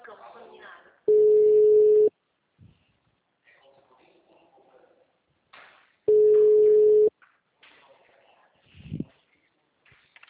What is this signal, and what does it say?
Ringback tone (Europe, 425 Hz), with speech.
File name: out- -2023- - - .23177